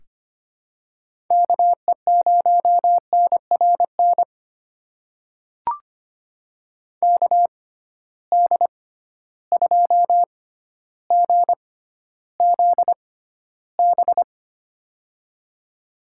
Morse code, heard as a keyed tone, about 25 wpm, spaced as for 13 wpm, 700 Hz. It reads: KE0NRN KD2GZB
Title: USA Call Signs - 6x Character Spacing - 25wpm